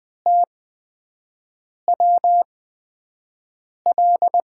Morse code, heard as a keyed tone, about 20 wpm, spaced as for 5 wpm, 700 Hz. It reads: TWL
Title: USA Call Signs - 6 Characters - 8x Character Spacing - 20wpm